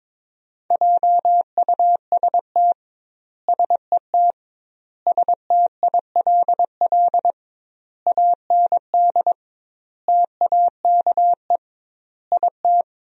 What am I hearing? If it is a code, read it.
Morse code, heard as a keyed tone, about 22 wpm, 700 Hz. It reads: JUST SET STILL AND TAKE IT
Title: 2x Word-Spacing - Sentences from Top 300 Words - 22wpm